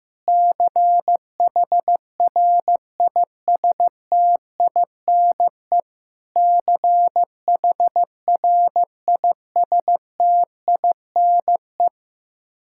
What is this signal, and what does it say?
Morse code, heard as a keyed tone, about 16 wpm, 700 Hz. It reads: CHRISTINE CHRISTINE
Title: Practice QSO - Exchange 1 - Beginning - 15wpm